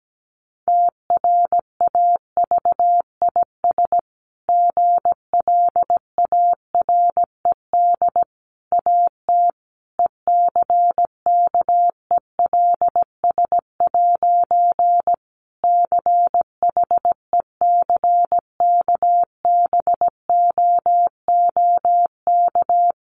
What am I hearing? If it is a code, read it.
Morse code, heard as a keyed tone, about 17 wpm, 700 Hz. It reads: TRAVIS GLARED AT ECKELS' CHECKBOOK
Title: A Sound Of Thunder 17wpm